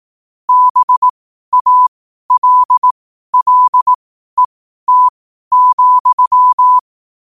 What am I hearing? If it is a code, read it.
Morse code, heard as a keyed tone, about 18 wpm, spaced as for 12 wpm, 1000 Hz. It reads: BALLET,
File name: qotdfiles / 12wpm